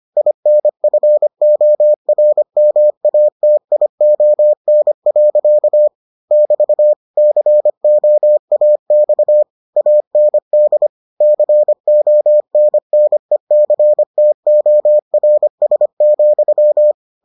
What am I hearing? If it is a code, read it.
Morse code, heard as a keyed tone, about 25 wpm, 600 Hz. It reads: INFORMATION. <BT> COAX AND CONNECTORS,